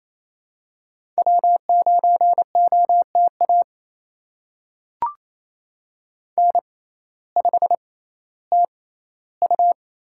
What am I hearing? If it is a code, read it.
Morse code, heard as a keyed tone, about 28 wpm, spaced as for 15 wpm, 700 Hz. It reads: W9OTA N5TU
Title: USA Call Signs - 5 Characters - 6x Character Spacing - 28wpm